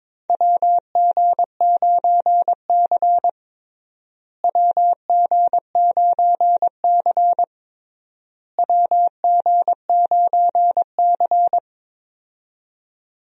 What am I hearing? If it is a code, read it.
Morse code, heard as a keyed tone, about 22 wpm, 700 Hz. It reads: WG9C WG9C WG9C
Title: USA Call Signs - 4 Characters - Repeated 3x - 22wpm